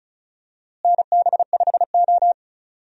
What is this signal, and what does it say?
Morse code, heard as a keyed tone, about 36 wpm, 700 Hz. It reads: NB5O
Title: USA Call Signs - 4 Characters - Repeated 3x - 35wpm